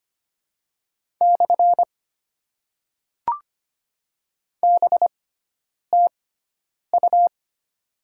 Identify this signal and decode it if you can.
Morse code, heard as a keyed tone, about 25 wpm, 700 Hz. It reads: / B T U
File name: Top 90 QSO Elements - 6x Character Spacing - 25wpm